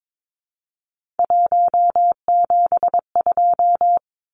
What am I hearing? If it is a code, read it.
Morse code, heard as a keyed tone, about 22 wpm, 700 Hz. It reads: 172
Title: Numbers - 3-Digit 22wpm